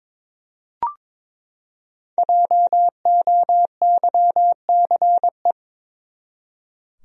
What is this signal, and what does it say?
Morse code, heard as a keyed tone, about 22 wpm, 700 Hz. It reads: JOYCE
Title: USA Common Names - 22wpm